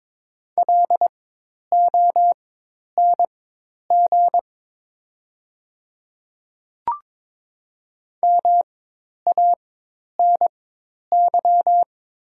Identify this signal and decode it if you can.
Morse code, heard as a keyed tone, about 22 wpm, spaced as for 10 wpm, 700 Hz. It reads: LONG MANY
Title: 4x Character Spacing - Top 100 Words - 22wpm